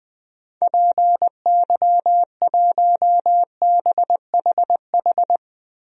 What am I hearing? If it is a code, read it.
Morse code, heard as a keyed tone, about 20 wpm, 700 Hz. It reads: PY1BHH
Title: DXCC-100 Call Signs - 6 Characters - Repeated 2x - 20wpm